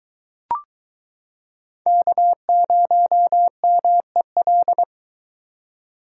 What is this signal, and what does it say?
Morse code, heard as a keyed tone, about 22 wpm, 700 Hz. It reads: K0MEL